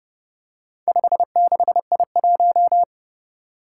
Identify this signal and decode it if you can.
Morse code, heard as a keyed tone, about 28 wpm, 700 Hz. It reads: 56I1
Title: Four Challenging Letter-Number Combinations 30wpm